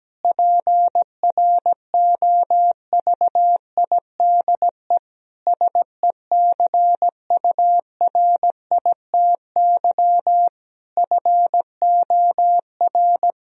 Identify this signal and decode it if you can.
Morse code, heard as a keyed tone, about 17 wpm, 700 Hz. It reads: PROVIDE SECURITY FOR
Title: Sentences from Top 1000 Words - Review Part 4 - 17wpm